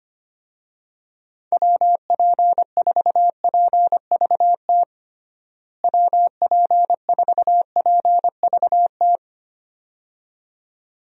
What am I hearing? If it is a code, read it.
Morse code, heard as a keyed tone, about 25 wpm, 700 Hz. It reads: WP4PVT WP4PVT
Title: USA Call Signs - 6 Characters - Repeated 2x - 25wpm